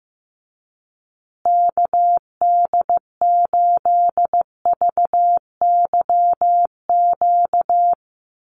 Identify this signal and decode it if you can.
Morse code, heard as a keyed tone, about 15 wpm, 700 Hz. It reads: KD8VYQ
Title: USA Call Signs Repeated 2x 15wpm